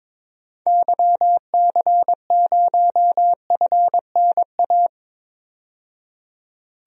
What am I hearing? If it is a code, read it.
Morse code, heard as a keyed tone, about 21 wpm, 700 Hz. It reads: YC0FNA
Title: DXCC-100 Call Signs - 6 Characters - 22wpm